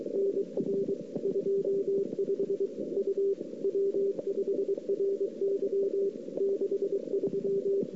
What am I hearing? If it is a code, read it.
Morse code, heard as a keyed tone, about 28 wpm, 410 Hz. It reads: NR 25 U W5RY 63
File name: CW / 2008-ss-cw-k5tr-op-n5rz -05:00